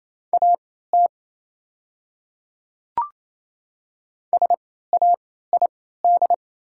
Morse code, heard as a keyed tone, about 28 wpm, spaced as for 16 wpm, 700 Hz. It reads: AT SAID
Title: 3x Character Spacing - Top 300 Words - 28wpm